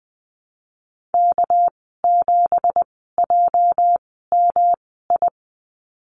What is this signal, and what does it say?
Morse code, heard as a keyed tone, about 20 wpm, spaced as for 15 wpm, 700 Hz. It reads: K7JMI
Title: 213 Lesson 34g - Number 8 - Call Signs 20wpm